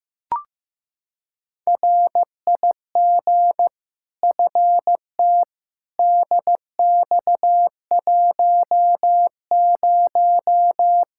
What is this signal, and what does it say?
Morse code, heard as a keyed tone, about 15 wpm, 700 Hz. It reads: RIG FT DX10
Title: Practice QSO - Exchange 2 - Info - 15wpm